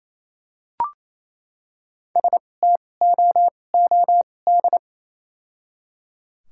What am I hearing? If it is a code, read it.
Morse code, heard as a keyed tone, about 28 wpm, spaced as for 20 wpm, 700 Hz. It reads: STOOD
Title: 2x Character Spacing - Top 500 Words - 28wpm